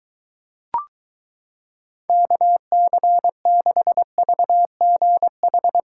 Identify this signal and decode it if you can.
Morse code, heard as a keyed tone, about 23 wpm, 700 Hz. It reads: KC6VGH